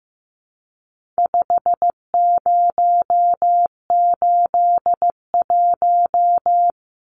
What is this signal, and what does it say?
Morse code, heard as a keyed tone, about 15 wpm, 700 Hz. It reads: 5081
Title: Numbers - 4-Digit 15wpm